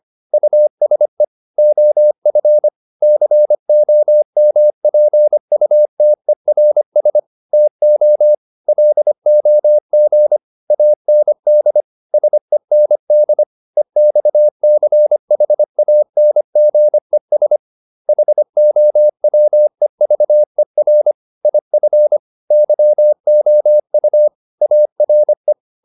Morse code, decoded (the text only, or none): USE OF COMPUTERS TO LOG AND SEND EXCHANGES HOWEVER IF YOU ARE